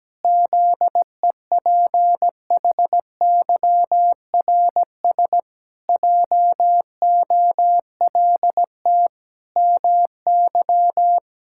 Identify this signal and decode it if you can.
Morse code, heard as a keyed tone, about 17 wpm, 700 Hz. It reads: ZEPHYRS JOLT MY
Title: Pangram Challenge v2 - 17wpm